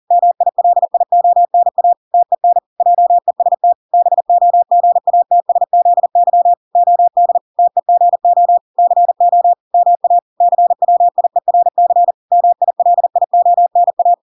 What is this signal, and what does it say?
Morse code, heard as a keyed tone, about 40 wpm, 700 Hz. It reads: MILIONA TEN JEST BOGATSZY OD TEGO CO MA CWIERC MILIONA